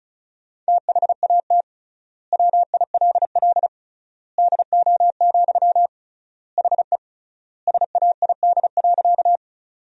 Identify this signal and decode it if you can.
Morse code, heard as a keyed tone, about 36 wpm, 700 Hz. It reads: THAT WILL DO, HE SAID.
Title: 3x Word-Spacing - Sentences from Top 100 Words - 35wpm